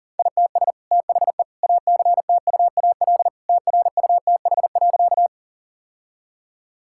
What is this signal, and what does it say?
Morse code, heard as a keyed tone, about 40 wpm, 700 Hz. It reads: ITS THE ACTUAL TRUTH.
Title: Sentences from Top 1000 Words - Review Part 5 - 40wpm